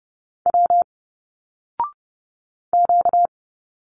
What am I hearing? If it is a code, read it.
Morse code, heard as a keyed tone, about 30 wpm, 700 Hz. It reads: W Q
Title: Single Letters 30wpm